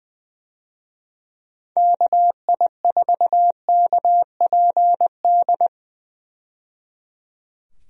Morse code, heard as a keyed tone, about 20 wpm, 700 Hz. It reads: KI4KPD
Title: USA Call Signs - 6 Characters - 20wpm